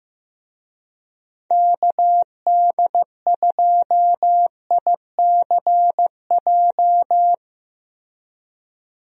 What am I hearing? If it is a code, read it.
Morse code, heard as a keyed tone, about 15 wpm, 700 Hz. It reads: KD2ICJ
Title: USA Call Signs - 6 Characters - Repeated 3x - 15wpm